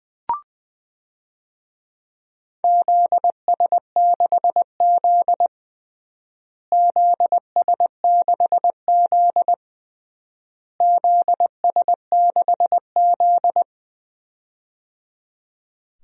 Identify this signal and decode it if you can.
Morse code, heard as a keyed tone, about 21 wpm, 700 Hz. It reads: ZS6Z ZS6Z ZS6Z
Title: DXCC-100 Call Signs - 4 Characters - Repeated 3x - 20wpm